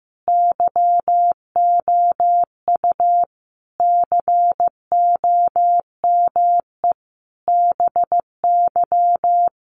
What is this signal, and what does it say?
Morse code, heard as a keyed tone, about 15 wpm, 700 Hz. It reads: YOU COME BY
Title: Sentences from Top 100 Words 15wpm